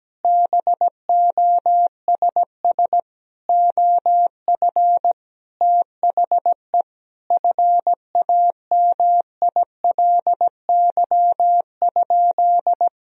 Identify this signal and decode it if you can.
Morse code, heard as a keyed tone, about 17 wpm, 700 Hz. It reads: BOSS OF THE FAMILY?
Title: Sentences from Top 1000 Words - Review Part 4 - 17wpm